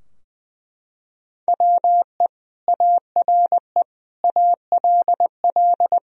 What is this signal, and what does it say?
Morse code, heard as a keyed tone, about 20 wpm, 700 Hz. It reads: WE ARE ALL
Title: Sentences from Top 1000 Words - Review Part 3 - 20wpm